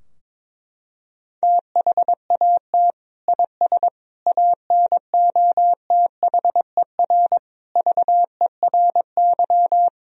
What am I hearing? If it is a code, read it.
Morse code, heard as a keyed tone, about 22 wpm, 700 Hz. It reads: THAT IS ANOTHER VERY
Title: Sentences from Top 1000 Words - Review Part 2 - 22wpm